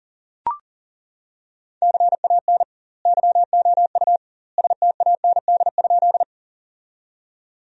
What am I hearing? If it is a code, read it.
Morse code, heard as a keyed tone, about 37 wpm, 700 Hz. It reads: CAN YOU STAND?
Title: 2x Word-Spacing - Sentences from Top 300 Words - 40wpm